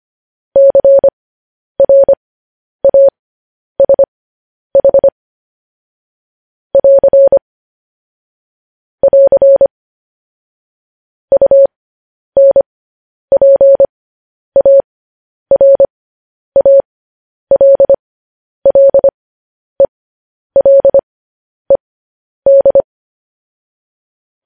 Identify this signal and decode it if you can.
Morse code, heard as a keyed tone, about 25 wpm, spaced as for 10 wpm, 550 Hz. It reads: CRASH <AR> <AR> UNPARALLELED
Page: news-headlines-in-morse-code-at-10-wpm